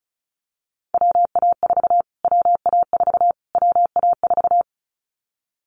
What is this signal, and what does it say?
Morse code, heard as a keyed tone, about 34 wpm, 700 Hz. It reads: WA4 WA4 WA4
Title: Callsign Prefixes for Top 25 DXCC - Repeated 3x - 35wpm